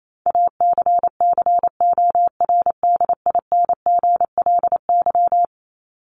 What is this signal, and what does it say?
Morse code, heard as a keyed tone, about 28 wpm, 700 Hz. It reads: ACCORDINGLY
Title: LY Suffix - Top 100 Words with Base Words 28wpm